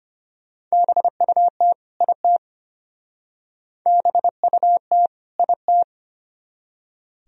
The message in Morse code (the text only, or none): BUT IT BUT IT